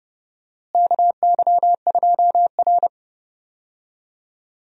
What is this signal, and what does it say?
Morse code, heard as a keyed tone, about 32 wpm, 700 Hz. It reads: KY2R
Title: USA Call Signs - 4 Characters - Repeated 3x - 30wpm